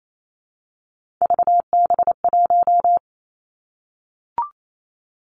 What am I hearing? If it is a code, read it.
Morse code, heard as a keyed tone, about 28 wpm, 700 Hz. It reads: VB1
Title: Three Challenging Letter-Number Combinations 28wpm